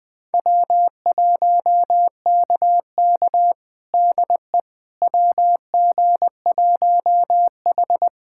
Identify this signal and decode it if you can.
Morse code, heard as a keyed tone, about 21 wpm, 700 Hz. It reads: W1KK DE WG1H
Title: USA Callsign DE Callsign - 4 Characters - 20wpm